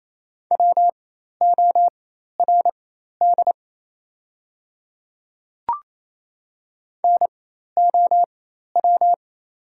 Morse code, heard as a keyed tone, about 28 wpm, spaced as for 13 wpm, 700 Hz. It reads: WORD NOW
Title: 4x Character Spacing - Top 200 Words - 28wpm